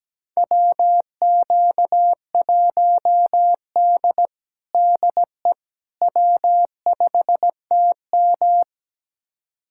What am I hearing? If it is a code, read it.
Morse code, heard as a keyed tone, about 17 wpm, 700 Hz. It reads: WQ1D DE W5TM
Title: USA Callsign DE Callsign - 4 Characters - 17wpm